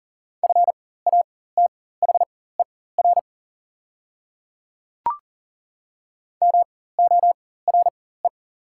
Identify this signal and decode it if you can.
Morse code, heard as a keyed tone, about 41 wpm, spaced as for 19 wpm, 700 Hz. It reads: FATHER MORE
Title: 4x Character Spacing - Top 200 Words - 40wpm